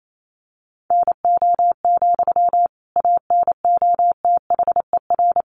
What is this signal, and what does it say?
Morse code, heard as a keyed tone, about 28 wpm, 700 Hz. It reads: NO, ANOTHER
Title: Sentences from Top 600 Words 28wpm